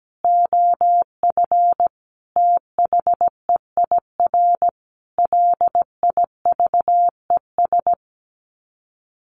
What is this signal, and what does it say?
Morse code, heard as a keyed tone, about 17 wpm, 700 Hz. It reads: OF THEIR LIVES
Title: Sets of 5 Words Encore 17wpm